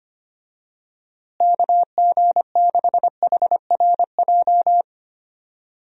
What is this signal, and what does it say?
Morse code, heard as a keyed tone, about 25 wpm, 700 Hz. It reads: KG6HRJ
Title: USA Call Signs - 6 Characters - Repeated 3x - 25wpm